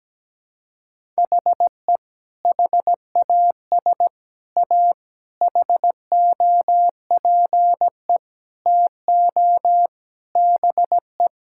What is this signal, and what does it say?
Morse code, heard as a keyed tone, about 17 wpm, 700 Hz. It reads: HE HAS A HOPE TO BE
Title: Sentences from Top 1000 Words - Review Part 6 - 17wpm